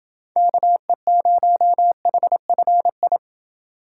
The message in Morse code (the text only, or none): KE0HFI